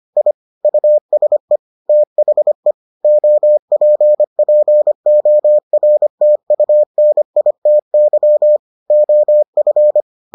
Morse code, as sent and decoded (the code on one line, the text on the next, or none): .. / ..- ... / - .... / --- .--. .--. --- .-. - ..- -. .. - -.-- / --- ..-.
I USE THE OPPORTUNITY OF